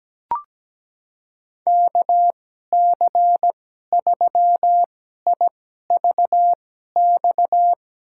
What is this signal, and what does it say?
Morse code, heard as a keyed tone, about 17 wpm, spaced as for 12 wpm, 700 Hz. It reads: KC3IVX
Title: USA Call Signs - 6 Characters - 2x Character Spacing - 17wpm